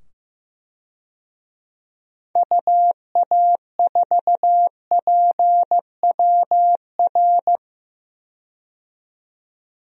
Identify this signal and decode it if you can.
Morse code, heard as a keyed tone, about 15 wpm, 700 Hz. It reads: UA4PWR
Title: DXCC-100 Call Signs - 6 Characters - 15wpm